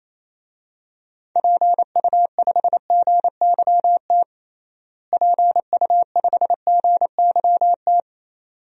Morse code, heard as a keyed tone, about 28 wpm, 700 Hz. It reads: PU5GYT PU5GYT
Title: DXCC-100 Call Signs - 6 Characters - Repeated 2x - 28wpm